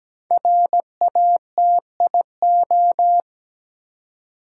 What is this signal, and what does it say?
Morse code, heard as a keyed tone, about 17 wpm, 700 Hz. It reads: RATIO
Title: Five Letter Words - 17wpm